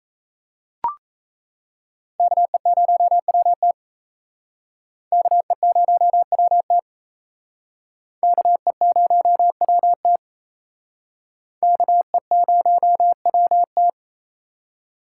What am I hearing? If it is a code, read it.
Morse code, heard as a keyed tone, about 35 wpm, 700 Hz. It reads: KE0WT KE0WT KE0WT KE0WT